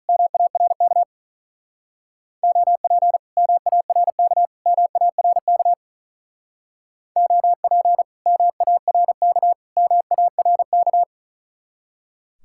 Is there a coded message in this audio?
Yes: Morse code from 0.1 to 11.0 s, heard as a keyed tone, about 47 wpm, 700 Hz. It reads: MARK OP MARK MARK OP MARK MARK